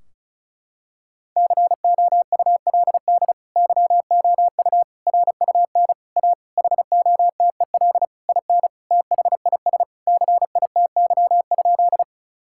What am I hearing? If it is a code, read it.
Morse code, heard as a keyed tone, about 36 wpm, 700 Hz. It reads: COULD YOU RUN A HOTEL IN THIS CITY?
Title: Sentences from Top 1600 Words 35wpm